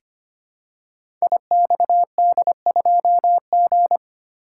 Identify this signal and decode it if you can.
Morse code, heard as a keyed tone, about 25 wpm, 700 Hz. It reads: IXD2G
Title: Code Groups - 5 Letter:Numbers - 25wpm